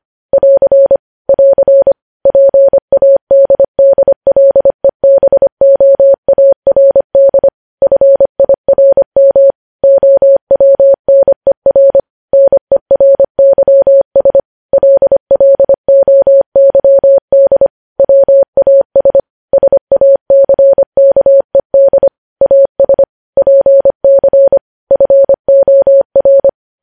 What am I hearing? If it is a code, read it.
Morse code, heard as a keyed tone, about 25 wpm, 550 Hz. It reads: <AR> <AR> PADDLEBOARD FIRM OWNER NERYS LLOYD WAS SACKED AS PC FOR
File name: podcast / News headlines-1x-25x25